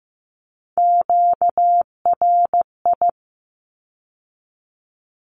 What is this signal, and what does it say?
Morse code, heard as a keyed tone, about 15 wpm, 700 Hz. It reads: QRI
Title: Q-Codes with Spoken Meaning - 15wpm